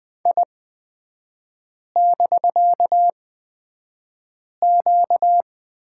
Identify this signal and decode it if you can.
Morse code, heard as a keyed tone, about 20 wpm, spaced as for 5 wpm, 700 Hz. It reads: I<BK>Q
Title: 250 Lesson 40e - BK - Three Characters 20wpm